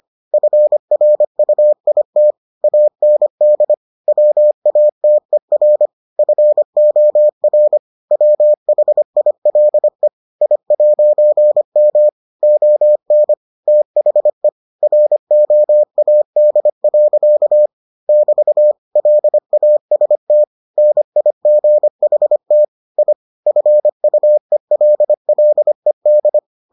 Morse code, heard as a keyed tone, about 25 wpm, 600 Hz. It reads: FRUIT AND WATER FOR WHILE I'M ON THE ROAD. <BT> LAST NIGHT I FUELLED